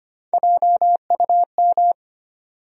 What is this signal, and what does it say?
Morse code, heard as a keyed tone, about 25 wpm, 700 Hz. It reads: JUM
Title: USA Call Sign Postfixes - 3 Letters 25wpm